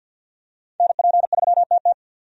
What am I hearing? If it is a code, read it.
Morse code, heard as a keyed tone, about 52 wpm, 700 Hz. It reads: NP3TT
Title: USA Call Signs - 5 Characters - 50wpm